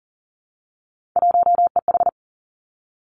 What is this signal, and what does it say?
Morse code, heard as a keyed tone, about 42 wpm, 700 Hz. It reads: 1EH
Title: Three Challenging Letter-Number Combinations 40wpm